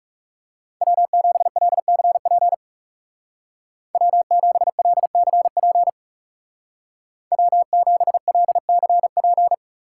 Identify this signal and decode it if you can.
Morse code, heard as a keyed tone, about 37 wpm, 700 Hz. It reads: W7LCP W7LCP W7LCP
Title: Speed Racing - USA 1x3 Call Signs 30wpm